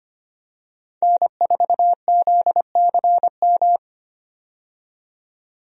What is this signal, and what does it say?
Morse code, heard as a keyed tone, about 25 wpm, 690 Hz. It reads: N4ZCM
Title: USA Call Signs - 5 Characters - 25wpm